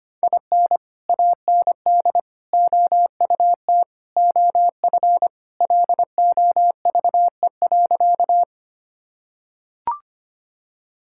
Morse code, heard as a keyed tone, about 25 wpm, 700 Hz. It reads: IN AND OUT OF LOVE.
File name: practice / 25wpm / Sentences with 4-Word Phrases - 25wpm